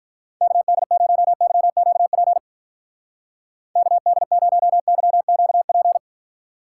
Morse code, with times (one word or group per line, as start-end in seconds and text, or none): 0.4-2.4 KD0YQP
3.8-6.0 KD0YQP